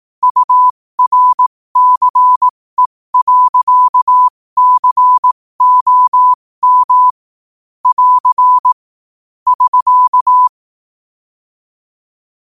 Morse code, heard as a keyed tone, about 18 wpm, spaced as for 15 wpm, 1000 Hz. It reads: URCE.COM <AR> <SK>
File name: qotdfiles / 15wpm